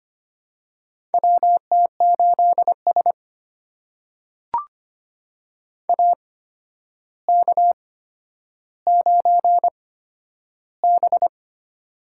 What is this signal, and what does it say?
Morse code, heard as a keyed tone, about 25 wpm, spaced as for 11 wpm, 700 Hz. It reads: WT8S AK9B
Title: USA Call Signs - 4 Characters - 8x Character Spacing - 25wpm